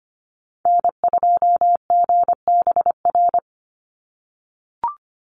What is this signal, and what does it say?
Morse code, heard as a keyed tone, about 25 wpm, 700 Hz. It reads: N2GBR
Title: NA SOTA Activator Callsigns 25wpm